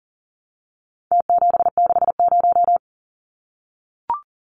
Morse code, heard as a keyed tone, about 40 wpm, 700 Hz. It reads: T760
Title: Numbers - Contest Style - 4-Digit 40wpm